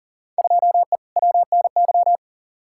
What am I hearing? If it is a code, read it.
Morse code, heard as a keyed tone, about 39 wpm, 700 Hz. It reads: 2E WNY
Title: Field Day Exchanges Encore - 40wpm